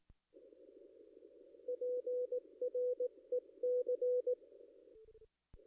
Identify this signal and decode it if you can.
Morse code, heard as a keyed tone, about 17 wpm, 500 Hz. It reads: PREC
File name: contests / ss-cw-k5tr-op-n5rz -2200